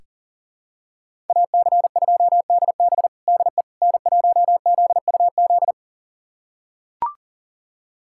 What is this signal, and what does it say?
Morse code, heard as a keyed tone, about 40 wpm, 700 Hz. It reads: AC2DB DE N1ZUZ
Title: USA Callsign DE Callsign - 5 Characters - 40wpm